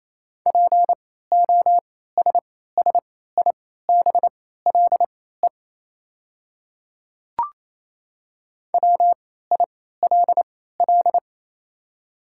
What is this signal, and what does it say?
Morse code, heard as a keyed tone, about 28 wpm, spaced as for 16 wpm, 700 Hz. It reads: POSSIBLE WILL
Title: 3x Character Spacing - Top 500 Words - 28wpm